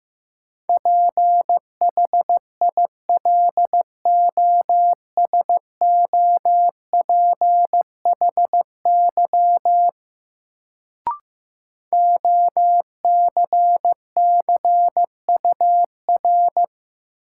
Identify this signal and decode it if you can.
Morse code, heard as a keyed tone, about 15 wpm, 700 Hz. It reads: PHILOSOPHY OCCUR
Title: Sentences from Top 900 Words - New Word Drill - 15wpm